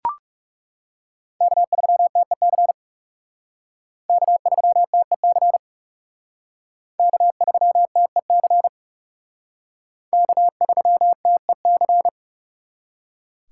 Morse code, heard as a keyed tone, about 37 wpm, 700 Hz. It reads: K3TEC K3TEC K3TEC K3TEC
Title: Speed Racing - USA 1x3 Call Signs 30wpm